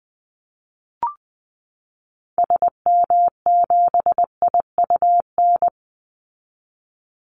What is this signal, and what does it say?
Morse code, heard as a keyed tone, about 20 wpm, 700 Hz. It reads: SM7IUN
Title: Call Signs Repeated 3x - 20wpm